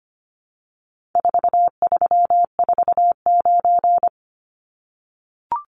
Morse code, heard as a keyed tone, about 25 wpm, 700 Hz. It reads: 4349
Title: Numbers - 4-Digit 25wpm